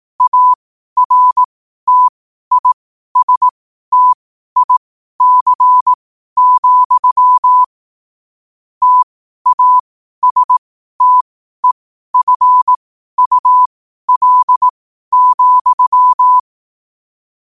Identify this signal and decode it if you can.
Morse code, heard as a keyed tone, about 18 wpm, spaced as for 12 wpm, 1000 Hz. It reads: ARTISTIC, TASTEFUL,
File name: qotdfiles / 12wpm